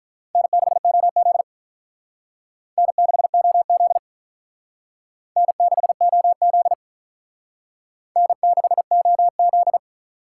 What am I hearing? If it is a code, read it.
Morse code, heard as a keyed tone, about 42 wpm, 700 Hz. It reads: N6OZ N6OZ N6OZ N6OZ
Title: Speed Racing - USA 1x2 Call Signs 35wpm